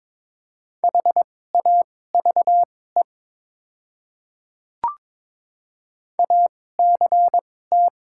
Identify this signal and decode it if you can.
Morse code, heard as a keyed tone, about 22 wpm, spaced as for 16 wpm, 700 Hz. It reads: HAVE ACT
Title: 2x Character Spacing - Top 300 Words - 22wpm